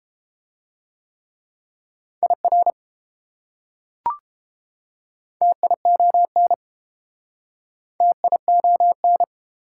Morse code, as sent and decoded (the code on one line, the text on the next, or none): .. .-. / - .. --- -. / - .. --- -.
IR TION TION